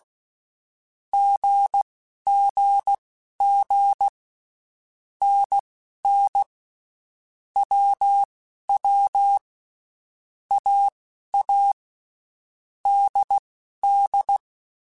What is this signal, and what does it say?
Morse code, heard as a keyed tone, about 16 wpm, spaced as for 11 wpm, 800 Hz. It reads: GGG NN WW AA DD
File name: Basic Letter Groups By7Groups / AWUNDG / AWUNDG@10